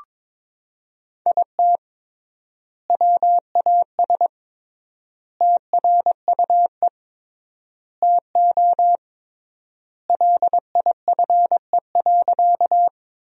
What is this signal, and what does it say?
Morse code, heard as a keyed tone, about 22 wpm, 700 Hz. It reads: IT WAS TRUE TO LIFE.
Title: 3x Word-Spacing - Sentences from Top 400 Words - 22wpm